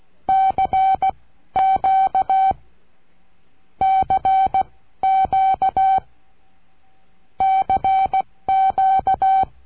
Morse code, heard as a keyed tone, about 16 wpm, spaced as for 13 wpm, 770 Hz. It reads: CQ CQ CQ
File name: cursos / CURSO DE TELEGRAFIA / Telegrafia - Trilha 22